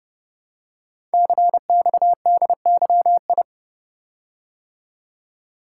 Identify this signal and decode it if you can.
Morse code, heard as a keyed tone, about 30 wpm, 700 Hz. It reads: CXDYI
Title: Code Groups - 5 Letters - 30wpm